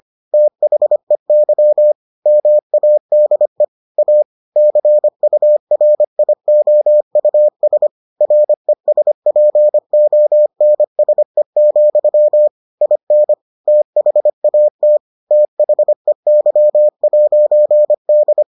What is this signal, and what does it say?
Morse code, heard as a keyed tone, about 25 wpm, 600 Hz. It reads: THEY MADE A CURIOUS RESPONSE, IN THAT THEY'D